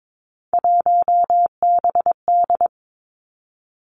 Morse code, heard as a keyed tone, about 22 wpm, 700 Hz. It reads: 1BD
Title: Three Challenging Letter-Number Combinations 22wpm